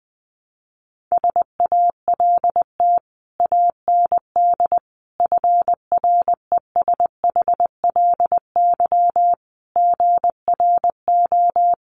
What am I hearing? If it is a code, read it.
Morse code, heard as a keyed tone, about 20 wpm, 700 Hz. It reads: SALT AND FRESHLY GRO
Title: Sets of 5 Words Encore 20wpm